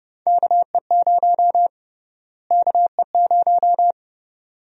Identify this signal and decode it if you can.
Morse code, heard as a keyed tone, about 32 wpm, 700 Hz. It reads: KE0 KE0
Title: USA Call Sign Prefixes - Repeated 2x - 30wpm